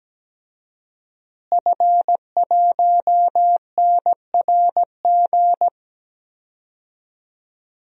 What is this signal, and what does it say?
Morse code, heard as a keyed tone, about 17 wpm, 700 Hz. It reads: F1NRG
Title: DXCC-100 Call Signs - 5 Characters - Repeated 3x - 17wpm